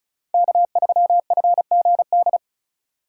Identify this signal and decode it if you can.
Morse code, heard as a keyed tone, about 36 wpm, 700 Hz. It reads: K3FGD
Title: USA Call Signs - 3x Character Spacing - 35wpm